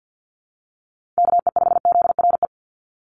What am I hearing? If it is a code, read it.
Morse code, heard as a keyed tone, about 48 wpm, 700 Hz. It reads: KE5ZRE
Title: USA Call Signs 50wpm